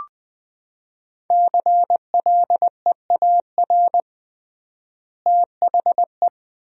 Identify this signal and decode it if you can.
Morse code, heard as a keyed tone, about 20 wpm, 700 Hz. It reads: CLEAR THE
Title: 3x Word-Spacing - Sentences from Top 500 Words - 20wpm